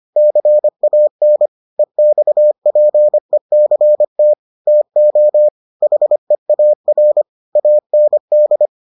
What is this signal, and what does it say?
Morse code, heard as a keyed tone, about 25 wpm, 600 Hz. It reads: CAN EXPECT TO HEAR AND